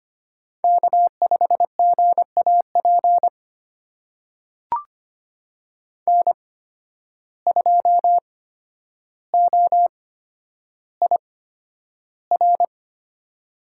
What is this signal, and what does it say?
Morse code, heard as a keyed tone, about 25 wpm, spaced as for 11 wpm, 700 Hz. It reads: K5GAP N2OIR
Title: USA Call Signs - 5 Characters - 8x Character Spacing - 25wpm